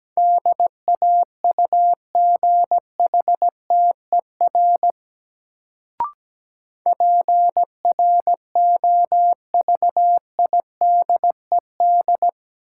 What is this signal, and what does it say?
Morse code, heard as a keyed tone, about 17 wpm, 700 Hz. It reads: DAUGHTER PROVIDED
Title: Sentences from Top 1500 Words - New Word Drill - 17wpm